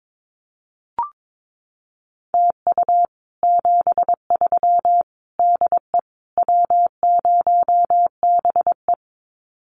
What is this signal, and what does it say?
Morse code, heard as a keyed tone, about 22 wpm, 700 Hz. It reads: TU 73 DE W0BE
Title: Common QSO Phrases Encore 22wpm